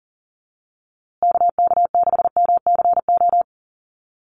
Morse code, heard as a keyed tone, about 40 wpm, 700 Hz. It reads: KK6MCO
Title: USA Call Signs Repeated 3x 40wpm